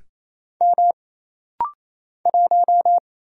Morse code, heard as a keyed tone, about 28 wpm, 700 Hz. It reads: M 1
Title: Single Letter-Number - Rapid-Fire 28wpm